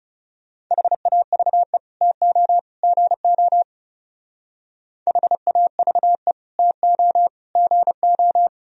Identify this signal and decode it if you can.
Morse code, heard as a keyed tone, about 36 wpm, 700 Hz. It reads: HAVE TO GO HAVE TO GO
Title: Speed-Racing - Sets of 3 Words 30wpm